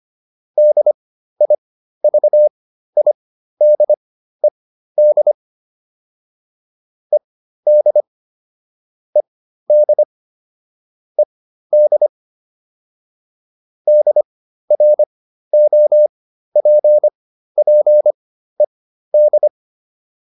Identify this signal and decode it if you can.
Morse code, heard as a keyed tone, about 25 wpm, spaced as for 13 wpm, 600 Hz. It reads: DIVIDED ED ED ED DROPPED